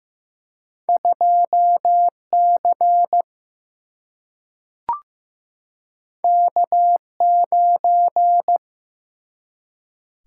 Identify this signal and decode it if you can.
Morse code, heard as a keyed tone, about 15 wpm, 700 Hz. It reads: 2C K9
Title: Single-Letter Single-Number 15wpm